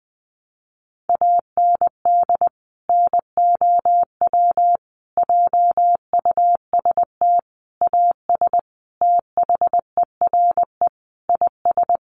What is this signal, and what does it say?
Morse code, heard as a keyed tone, about 20 wpm, 700 Hz. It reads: AND NOW JUST AS THERE IS
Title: Sentences from Top 1200 Words 20wpm